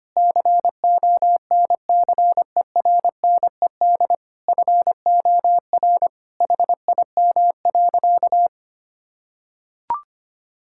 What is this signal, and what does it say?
Morse code, heard as a keyed tone, about 25 wpm, 700 Hz. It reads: CONCERNED FOR HIM.
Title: Sentences from Top 1000 Words - Review Part 6 - 25wpm